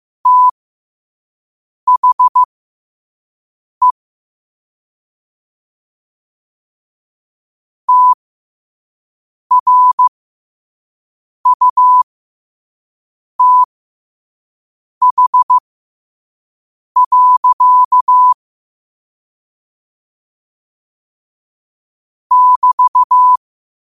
Morse code, heard as a keyed tone, about 15 wpm, spaced as for 5 wpm, 1000 Hz. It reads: THE TRUTH. <BT>
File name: qotdfiles / 5wpm